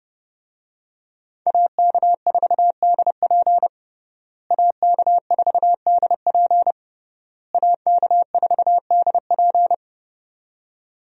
Morse code, heard as a keyed tone, about 30 wpm, 700 Hz. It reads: AK4DP AK4DP AK4DP
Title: USA Call Signs - 5 Characters - Repeated 3x - 30wpm